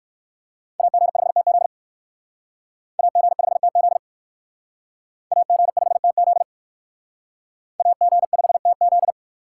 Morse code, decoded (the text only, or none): AG5TZ AG5TZ AG5TZ AG5TZ